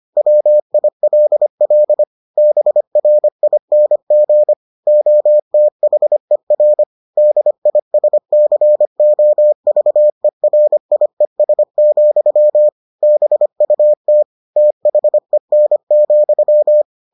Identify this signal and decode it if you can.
Morse code, heard as a keyed tone, about 25 wpm, 600 Hz. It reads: WILL BRING OTHER DISCOVERIES, BUT THEN,